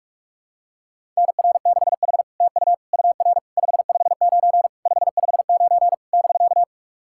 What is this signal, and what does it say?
Morse code, heard as a keyed tone, about 44 wpm, 700 Hz. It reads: NR6H TU UR 559 559 <BK>
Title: SOTA QSO Practice 45wpm